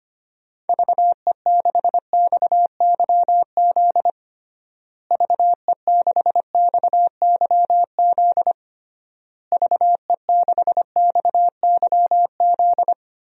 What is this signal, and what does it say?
Morse code, heard as a keyed tone, about 25 wpm, 700 Hz. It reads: VE6XYZ VE6XYZ VE6XYZ
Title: DXCC-100 Call Signs - 6 Characters - Repeated 3x - 25wpm